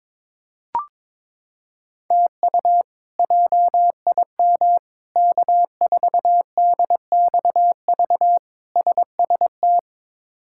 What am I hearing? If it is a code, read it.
Morse code, heard as a keyed tone, about 22 wpm, 700 Hz. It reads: TU JIM K4DXV SST
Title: K1USN SST QSO Practice - 22wpm